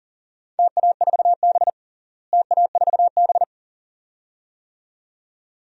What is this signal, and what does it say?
Morse code, heard as a keyed tone, about 37 wpm, 700 Hz. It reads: TA4B TA4B
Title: DXCC-100 Call Signs - 4 Characters - Repeated 2x - 40wpm